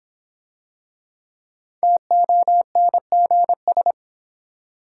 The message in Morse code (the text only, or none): TONGS